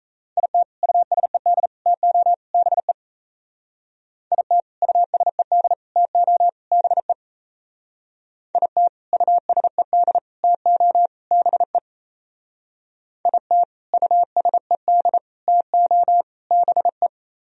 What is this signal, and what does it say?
Morse code, heard as a keyed tone, about 36 wpm, 700 Hz. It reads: IT USED TO BE IT USED TO BE IT USED TO BE IT USED TO BE